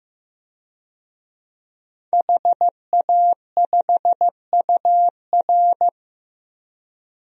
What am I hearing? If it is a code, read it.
Morse code, heard as a keyed tone, about 16 wpm, 700 Hz. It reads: HA5UR
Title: DXCC-100 Call Signs - 5 Characters - Repeated 2x - 15wpm